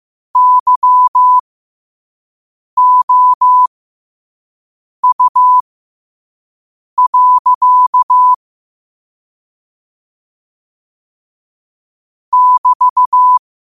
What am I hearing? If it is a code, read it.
Morse code, heard as a keyed tone, about 15 wpm, spaced as for 5 wpm, 1000 Hz. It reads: YOU. <BT>
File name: qotdfiles / 5wpm